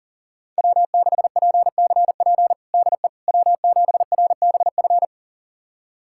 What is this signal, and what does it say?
Morse code, heard as a keyed tone, about 40 wpm, 700 Hz. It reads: W6PCP DE W7RBF
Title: USA Callsign DE Callsign - 5 Characters - 40wpm